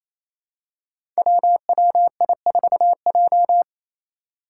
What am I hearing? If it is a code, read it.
Morse code, heard as a keyed tone, about 28 wpm, 700 Hz. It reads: WWI4J
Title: Five Challenging Letter-Number Combinations 28wpm